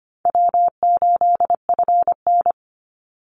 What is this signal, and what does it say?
Morse code, heard as a keyed tone, about 25 wpm, 700 Hz. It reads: W8FN
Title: Call Signs Repeated 2x - 25wpm